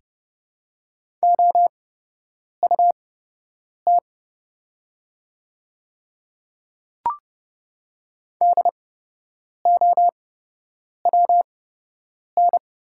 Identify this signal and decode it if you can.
Morse code, heard as a keyed tone, about 30 wpm, spaced as for 8 wpm, 700 Hz. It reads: OUT DOWN
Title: 8x Character Spacing - Top 300 Words - 30wpm